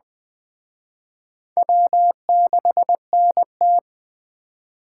Morse code, heard as a keyed tone, about 21 wpm, 700 Hz. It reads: W6NT